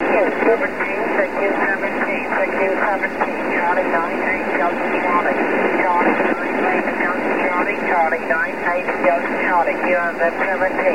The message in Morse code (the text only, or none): none